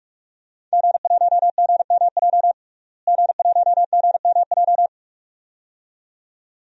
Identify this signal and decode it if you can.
Morse code, heard as a keyed tone, about 48 wpm, 700 Hz. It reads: G1GMJ G1GMJ
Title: DXCC-100 Call Signs - 5 Characters - Repeated 2x - 45wpm